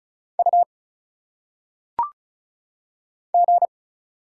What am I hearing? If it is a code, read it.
Morse code, heard as a keyed tone, about 35 wpm, 700 Hz. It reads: U G
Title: Single Letters - Phonetic Pronunciation 35wpm